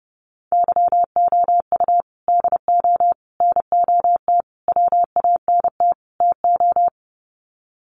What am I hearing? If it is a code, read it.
Morse code, heard as a keyed tone, about 30 wpm, 700 Hz. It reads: YOU DO NOT WANT TO
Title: Sets of 5 Words 30wpm